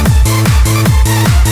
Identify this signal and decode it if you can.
There is no code.